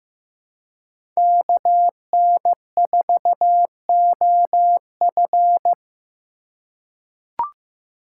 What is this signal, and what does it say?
Morse code, heard as a keyed tone, about 15 wpm, 700 Hz. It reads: KN4OF
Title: USA Call Signs - 5 Characters - 3x Character Spacing - 15wpm